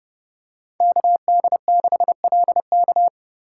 Morse code, heard as a keyed tone, about 32 wpm, 700 Hz. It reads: KD6LK
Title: USA Call Signs - 2x Character Spacing - 30wpm